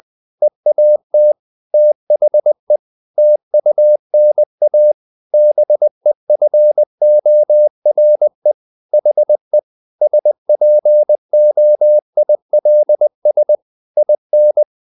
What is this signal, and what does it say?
Morse code, heard as a keyed tone, about 20 wpm, 600 Hz. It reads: EAT THE TUNA BEFORE HE SPOILS IN
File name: the old man and the sea / the old man and the sea part 1 20wpm